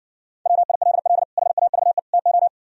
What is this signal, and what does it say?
Morse code, heard as a keyed tone, about 60 wpm, 700 Hz. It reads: WILL HAVE TO